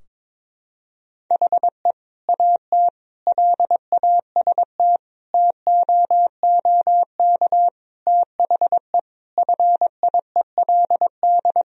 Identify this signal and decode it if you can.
Morse code, heard as a keyed tone, about 21 wpm, 700 Hz. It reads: HE AT LAST TOOK THE FIELD
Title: Sentences from Top 1000 Words - Review Part 2 - 22wpm